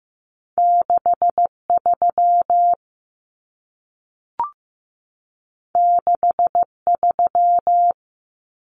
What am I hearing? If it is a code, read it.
Morse code, heard as a keyed tone, about 15 wpm, 700 Hz. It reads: 63 63
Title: Numbers - 2-Digit 15wpm